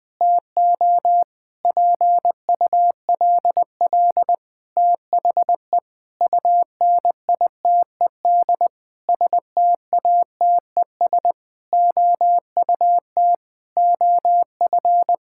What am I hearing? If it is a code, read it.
Morse code, heard as a keyed tone, about 20 wpm, 700 Hz. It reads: TO PULL THE UNITED STATES OUT OF
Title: Sentences with 8-Word Phrases - Spoken Beginning - 20wpm